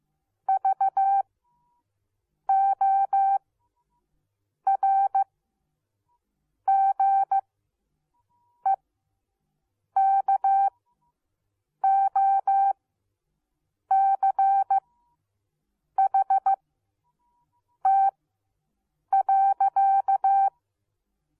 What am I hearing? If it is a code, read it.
Morse code, heard as a keyed tone, about 15 wpm, spaced as for 6 wpm, 780 Hz. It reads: VORGEKOCHT.